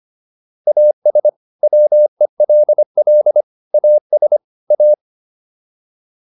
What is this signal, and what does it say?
Morse code, heard as a keyed tone, about 25 wpm, 600 Hz. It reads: AS WELL AS A